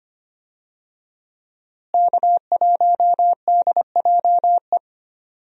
Morse code, heard as a keyed tone, about 25 wpm, 700 Hz. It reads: K1DJE
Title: USA Call Signs - 5 Characters - Repeated 3x - 25wpm